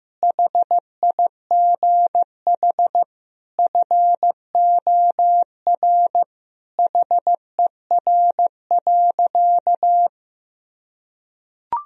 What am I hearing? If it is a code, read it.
Morse code, heard as a keyed tone, about 15 wpm, 700 Hz. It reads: HIGH FOR HER.
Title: Sentences from Top 1000 Words - Review Part 1 - 15wpm